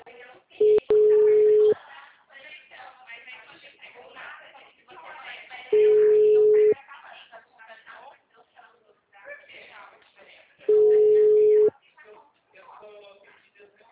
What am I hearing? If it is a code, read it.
Ringback tone (Europe, 425 Hz), with speech.
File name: out- -2036